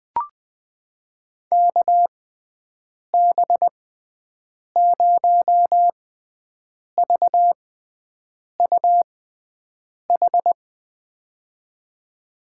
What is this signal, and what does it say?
Morse code, heard as a keyed tone, about 20 wpm, spaced as for 7 wpm, 700 Hz. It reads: KB0VUH
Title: USA Call Signs - 6 Characters - 6x Character Spacing - 20wpm